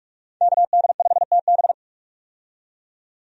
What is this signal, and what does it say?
Morse code, heard as a keyed tone, about 46 wpm, 700 Hz. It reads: KD5TB
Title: USA Call Signs - 5 Characters - 45wpm